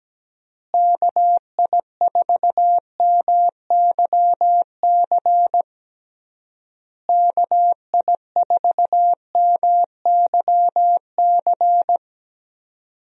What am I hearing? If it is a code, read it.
Morse code, heard as a keyed tone, about 17 wpm, 700 Hz. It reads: KI4MYC KI4MYC
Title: USA Call Signs - 6 Characters - Repeated 3x - 17wpm